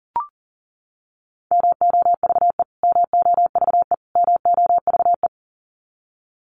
Morse code, heard as a keyed tone, about 40 wpm, 700 Hz. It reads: MOVE MOVE MOVE
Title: Top 500 Words Repeated 40wpm